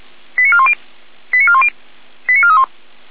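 Five-tone selective call (ZVEI), heard as quick runs of five tones, tones of about 70 ms. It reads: 89410 89410 89431
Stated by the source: ZVEI five-tone selective call received on SDR, Feuerwehr Orlamünde fire brigade (f85360).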